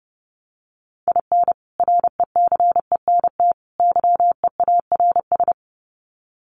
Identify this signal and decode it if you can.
Morse code, heard as a keyed tone, about 30 wpm, 700 Hz. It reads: IN RECENT YEARS
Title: Sets of 3 Words 30wpm